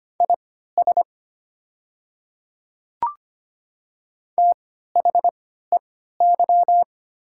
Morse code, heard as a keyed tone, about 25 wpm, spaced as for 14 wpm, 700 Hz. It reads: IS THEY
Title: 3x Character Spacing - Top 500 Words - 25wpm